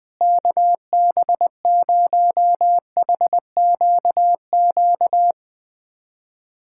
Morse code, heard as a keyed tone, about 21 wpm, 700 Hz. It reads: KB0HQQ
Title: USA Callsign DE Callsign - 6 Characters - 20wpm